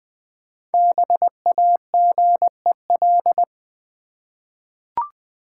Morse code, heard as a keyed tone, about 20 wpm, 700 Hz. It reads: BAGEL
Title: Easy Words with Hints - 20wpm